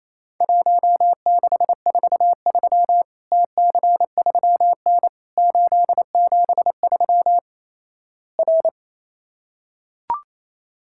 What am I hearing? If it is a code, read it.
Morse code, heard as a keyed tone, about 28 wpm, 700 Hz. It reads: 1643 TC3N 873 R
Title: Worked All Europe DX Contest - QTC Exchange - 28wpm